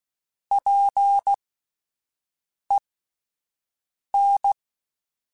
Morse code, heard as a keyed tone, about 16 wpm, spaced as for 5 wpm, 800 Hz. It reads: PEN